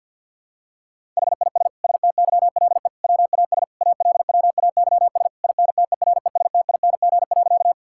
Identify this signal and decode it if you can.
Morse code, heard as a keyed tone, about 51 wpm, 700 Hz. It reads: HIS STYLE WAS ALWAYS INTERESTING.
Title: Sentences from Top 1000 Words - Review Part 5 - 50wpm